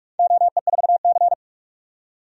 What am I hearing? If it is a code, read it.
Morse code, heard as a keyed tone, about 46 wpm, 700 Hz. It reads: OE4C
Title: DXCC-100 Call Signs - 4 Characters - 45wpm